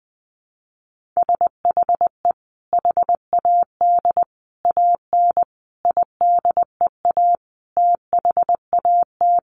Morse code, heard as a keyed tone, about 20 wpm, 700 Hz. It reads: SHE HAD AN IDEA THAT
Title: Sentences from Top 1400 Words 20wpm